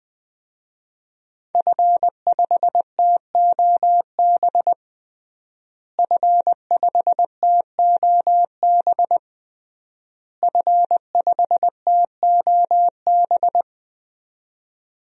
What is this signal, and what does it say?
Morse code, heard as a keyed tone, about 20 wpm, 700 Hz. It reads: F5TOB F5TOB F5TOB
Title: DXCC-100 Call Signs - 5 Characters - Repeated 3x - 20wpm